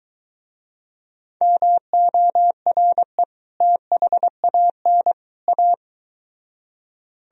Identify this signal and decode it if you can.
Morse code, heard as a keyed tone, about 23 wpm, 700 Hz. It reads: MORE THAN A